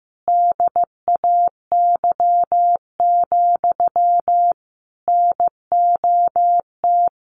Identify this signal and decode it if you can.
Morse code, heard as a keyed tone, about 15 wpm, 700 Hz. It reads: DAY, NOT
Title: Sentences from Top 1400 Words 15wpm